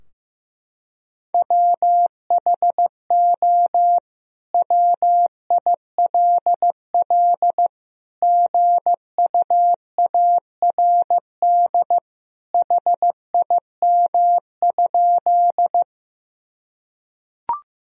Morse code, heard as a keyed tone, about 15 wpm, 700 Hz. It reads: WHO WILL GUARD HIM?
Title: Sentences from Top 1000 Words - Review Part 4 - 15wpm